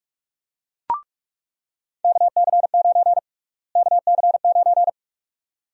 Morse code, heard as a keyed tone, about 46 wpm, 700 Hz. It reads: KC9 KC9
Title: USA Call Sign Prefixes - Repeated 2x - 45wpm